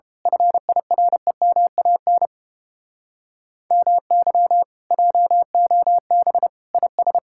Morse code, heard as a keyed tone, about 29 wpm, 700 Hz. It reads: FIREMAN MY JOB IS